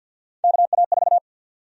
Morse code, heard as a keyed tone, about 47 wpm, 700 Hz. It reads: KA4
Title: USA Call Sign Prefixes - 50wpm